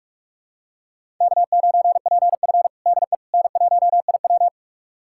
Morse code, heard as a keyed tone, about 42 wpm, 700 Hz. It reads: K9PF DE N1IW
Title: USA Callsign DE Callsign - 4 Characters - 45wpm